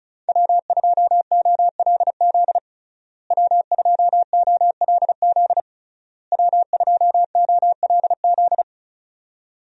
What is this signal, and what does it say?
Morse code, heard as a keyed tone, about 34 wpm, 700 Hz. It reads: W2OLZ W2OLZ W2OLZ
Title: USA Call Signs - 5 Characters - Repeated 3x - 35wpm